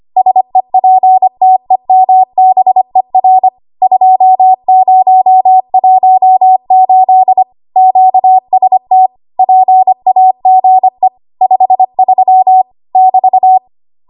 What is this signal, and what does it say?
Morse code, heard as a keyed tone, about 25 wpm, 750 Hz. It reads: SEPTEMBER 2018 QST PAGE 53 <BT>
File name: Archive / 25 WPM / 25WPM